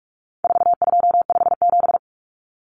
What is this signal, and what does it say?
Morse code, heard as a keyed tone, about 44 wpm, 700 Hz. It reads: 4257
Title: Numbers - 4-Digit 45wpm